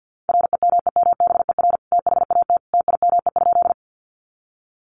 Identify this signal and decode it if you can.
Morse code, heard as a keyed tone, about 50 wpm, 700 Hz. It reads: REMEMBER THAT TIME?
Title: Sentences from Top 400 Words 50wpm